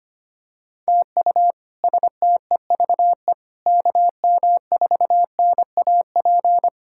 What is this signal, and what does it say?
Morse code, heard as a keyed tone, about 25 wpm, 700 Hz. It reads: TU STEVE KM4NAP
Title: K1USN SST QSO Practice - 25wpm